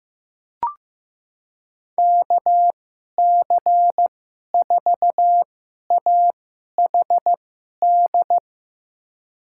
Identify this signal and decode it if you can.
Morse code, heard as a keyed tone, about 15 wpm, spaced as for 11 wpm, 700 Hz. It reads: KC4AHD
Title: USA Call Signs - 6 Characters - 2x Character Spacing - 15wpm